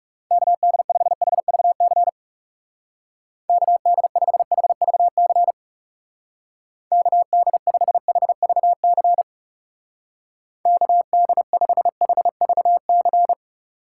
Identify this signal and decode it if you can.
Morse code, heard as a keyed tone, about 35 wpm, 700 Hz. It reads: KD5HVC KD5HVC KD5HVC KD5HVC
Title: Speed Racing - USA 2x3 Call Signs 30wpm